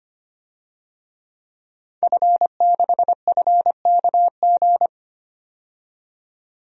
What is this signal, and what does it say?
Morse code, heard as a keyed tone, about 25 wpm, 700 Hz. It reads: F6FKG
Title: DXCC-100 Call Signs - 5 Characters - 25wpm